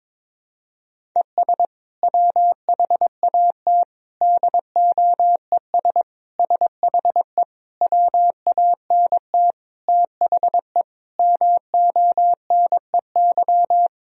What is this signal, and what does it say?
Morse code, heard as a keyed tone, about 22 wpm, 700 Hz. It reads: ES WHAT DOES SHE WANT THE MONEY
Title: Sentences from Top 400 Words and Abbreviations 22wpm